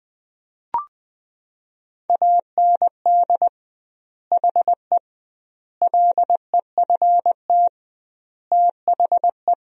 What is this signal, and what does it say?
Morse code, heard as a keyed tone, about 20 wpm, 700 Hz. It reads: AND HE LEFT THE
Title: 2x Word-Spacing - Sentences from Top 300 Words - 20wpm